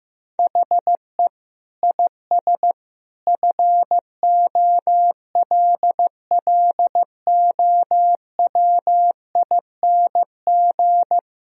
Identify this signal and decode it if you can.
Morse code, heard as a keyed tone, about 15 wpm, 700 Hz. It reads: HE IS FOLLOWING
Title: Sentences from Top 1000 Words - Review Part 4 - 15wpm